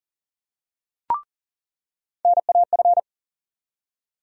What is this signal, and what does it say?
Morse code, heard as a keyed tone, about 38 wpm, 700 Hz. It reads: NAF
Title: USA Call Sign Postfixes - 3 Letters 40wpm